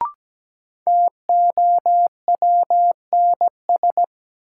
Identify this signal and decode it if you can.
Morse code, heard as a keyed tone, about 17 wpm, 700 Hz. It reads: TOWNS
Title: Sentences from Top 1000 Words - Review Part 6 - New Word Drill - 17wpm